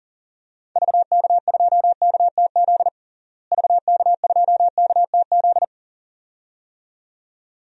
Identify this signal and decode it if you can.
Morse code, heard as a keyed tone, about 40 wpm, 700 Hz. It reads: VK2KTZ VK2KTZ
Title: DXCC-100 Call Signs - 6 Characters - Repeated 2x - 40wpm